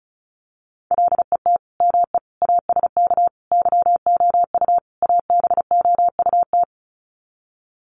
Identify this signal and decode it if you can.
Morse code, heard as a keyed tone, about 34 wpm, 700 Hz. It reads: LET ME ASK YOU ABOUT
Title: Sets of 5 Words 35wpm